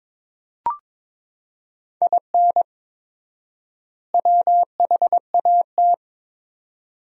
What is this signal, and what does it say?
Morse code, heard as a keyed tone, about 22 wpm, 700 Hz. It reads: IN WHAT
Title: 4x Word-Spacing - Sentences from Top 200 Words - 22wpm